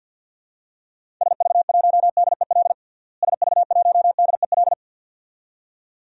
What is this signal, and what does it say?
Morse code, heard as a keyed tone, about 52 wpm, 700 Hz. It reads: SV1DEL SV1DEL
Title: DXCC-100 Call Signs - 6 Characters - Repeated 2x - 50wpm